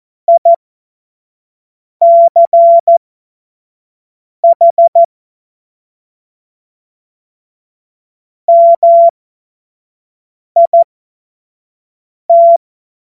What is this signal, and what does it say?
Morse code, heard as a keyed tone, about 14 wpm, spaced as for 5 wpm, 680 Hz. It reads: ICH MIT
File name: Witz-05-August-2025-25-F70